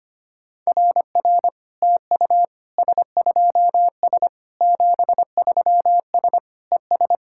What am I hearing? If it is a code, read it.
Morse code, heard as a keyed tone, about 25 wpm, 700 Hz. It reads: RR TU S2S 73S ES
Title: SOTA QSO Practice 25wpm